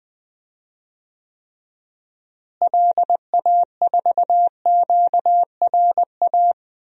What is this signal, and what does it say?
Morse code, heard as a keyed tone, about 20 wpm, 700 Hz. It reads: LA4QRA
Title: DXCC-100 Call Signs - 6 Characters - Repeated 3x - 20wpm